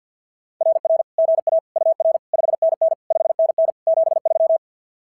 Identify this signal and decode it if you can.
Morse code, heard as a keyed tone, about 46 wpm, 660 Hz. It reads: RR GA UR 5NN 5NN 73
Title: SOTA QSO Practice 50wpm